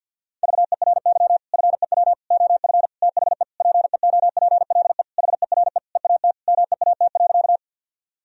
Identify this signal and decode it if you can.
Morse code, heard as a keyed tone, about 50 wpm, 700 Hz. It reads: VERY FEW OF THE PEOPLE HERE EAT MEAT.
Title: Sentences from Top 1000 Words - Review Part 3 - 50wpm